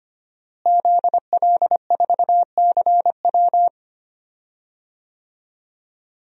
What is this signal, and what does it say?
Morse code, heard as a keyed tone, about 25 wpm, 700 Hz. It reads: ZL4CW
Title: DXCC-100 Call Signs - 5 Characters - Repeated 2x - 25wpm